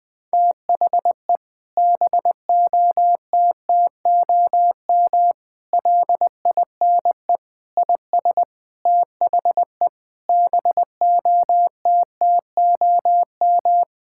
Morse code, heard as a keyed tone, about 20 wpm, 700 Hz. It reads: THE BOTTOM LINE IS THE BOTTOM
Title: Sentences from Top 1000 Words - Review Part 6 - 20wpm